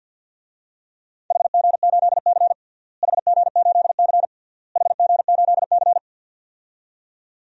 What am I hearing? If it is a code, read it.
Morse code, heard as a keyed tone, about 46 wpm, 700 Hz. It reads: HG8C HG8C HG8C
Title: DXCC-100 Call Signs - 4 Characters - Repeated 3x - 50wpm